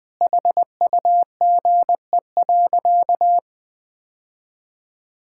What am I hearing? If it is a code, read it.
Morse code, heard as a keyed tone, about 21 wpm, 700 Hz. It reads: HUGE.
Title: Sentences with 6-Word Phrases - Spoken Beginning - 20wpm